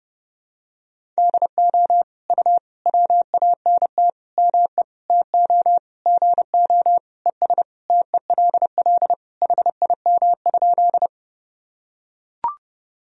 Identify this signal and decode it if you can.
Morse code, heard as a keyed tone, about 30 wpm, 700 Hz. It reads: DO U WANT ME TO GO ES TELL HIM?
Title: Sentences from Top 200 Words and Abbreviations 30wpm